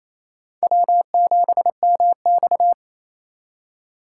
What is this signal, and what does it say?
Morse code, heard as a keyed tone, about 28 wpm, 700 Hz. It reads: W7MX
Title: USA Call Signs - 4 Characters - 2x Character Spacing - 28wpm